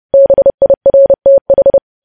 Morse code, heard as a keyed tone, about 30 wpm, 550 Hz. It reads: BIRTH
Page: news headlines in morse code at 30 wpm